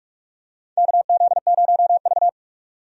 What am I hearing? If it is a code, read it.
Morse code, heard as a keyed tone, about 42 wpm, 700 Hz. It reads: KZ0V
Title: USA Call Signs - 4 Characters - 45wpm